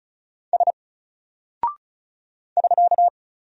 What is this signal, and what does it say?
Morse code, heard as a keyed tone, about 34 wpm, 700 Hz. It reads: S <SK>
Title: Single Character (Letter:Number:Punctuation:Prosign) 35wpm